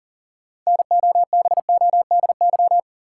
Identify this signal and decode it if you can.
Morse code, heard as a keyed tone, about 40 wpm, 700 Hz. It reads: NOBODY
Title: Sentences from Top 1600 Words - New Word Drill - 40wpm